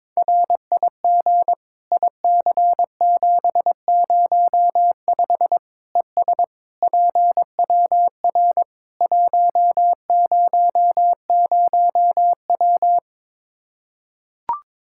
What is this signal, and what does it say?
Morse code, heard as a keyed tone, about 22 wpm, 700 Hz. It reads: RIG IC705 ES PWR 100W
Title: Practice QSO - Exchange 2 - Info - 22wpm